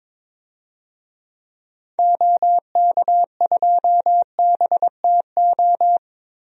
Morse code, heard as a keyed tone, about 21 wpm, 700 Hz. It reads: OK2BTO
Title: DXCC-100 Call Signs - 6 Characters - Repeated 2x - 22wpm